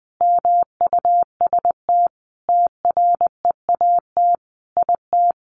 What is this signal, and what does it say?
Morse code, heard as a keyed tone, about 20 wpm, 700 Hz. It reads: MUST TREAT IT
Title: Sentences from Top 1300 Words 20wpm